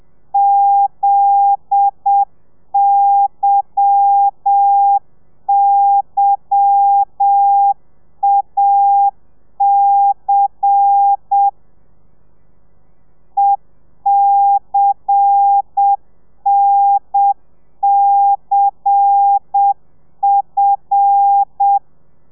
Morse code, heard as a keyed tone, about 7 wpm, 800 Hz. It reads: ZYYAC ECNCF
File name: telegrafia / i2tfj / Lez10 30